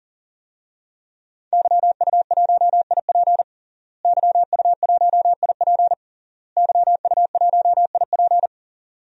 Morse code, heard as a keyed tone, about 40 wpm, 700 Hz. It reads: YU1IP YU1IP YU1IP
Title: DXCC-100 Call Signs - 5 Characters - Repeated 3x - 40wpm